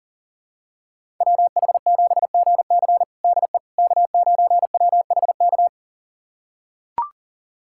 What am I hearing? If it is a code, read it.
Morse code, heard as a keyed tone, about 40 wpm, 700 Hz. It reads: WH7GC DE K9WHK
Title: USA Callsign DE Callsign - 5 Characters - 40wpm